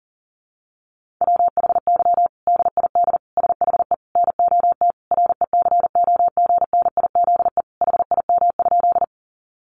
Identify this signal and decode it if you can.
Morse code, heard as a keyed tone, about 40 wpm, 700 Hz. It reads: WHY DID SHE NOT RECOGNIZE HIM?
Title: Sentences from Top 1400 Words 40wpm